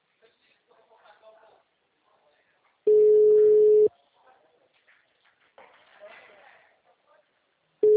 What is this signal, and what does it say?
Ringback tone (Europe, 425 Hz), with speech.